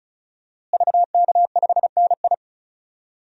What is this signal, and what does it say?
Morse code, heard as a keyed tone, about 36 wpm, 700 Hz. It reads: VK5NI
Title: DXCC-100 Call Signs - 5 Characters - Repeated 2x - 35wpm